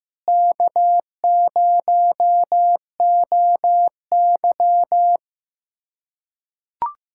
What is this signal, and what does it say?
Morse code, heard as a keyed tone, about 16 wpm, 700 Hz. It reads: K0OY
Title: USA Callsign DE Callsign - 4 Characters - 15wpm